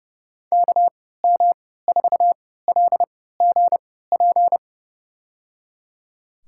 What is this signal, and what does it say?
Morse code, heard as a keyed tone, about 30 wpm, spaced as for 17 wpm, 700 Hz. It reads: KM4LGP
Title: USA Call Signs - 6 Characters - 3x Character Spacing - 30wpm